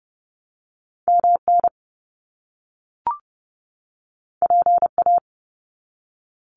Morse code, heard as a keyed tone, about 30 wpm, 700 Hz. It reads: MN PA
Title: US States 30wpm